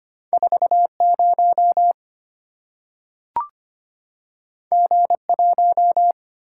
Morse code, heard as a keyed tone, about 25 wpm, 700 Hz. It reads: 40 G1
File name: Two Letter-Number Combinations - 25wpm